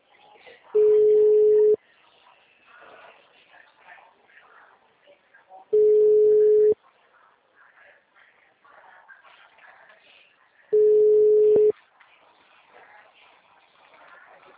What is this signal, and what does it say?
Ringback tone (Europe, 425 Hz), with speech.